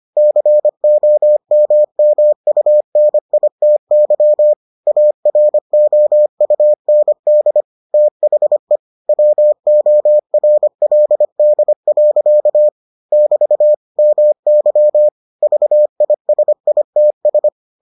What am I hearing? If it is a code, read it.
Morse code, heard as a keyed tone, about 25 wpm, 600 Hz. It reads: COMMUNITY AROUND THE WORLD. <BT> MY VISITS